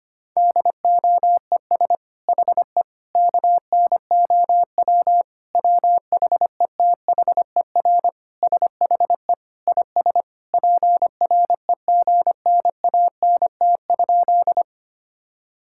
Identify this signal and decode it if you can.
Morse code, heard as a keyed tone, about 25 wpm, 700 Hz. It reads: DOES HE KNOW WHETHER SHE IS PREGNANT?
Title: Sentences from Top 1000 Words - Review Part 6 - 25wpm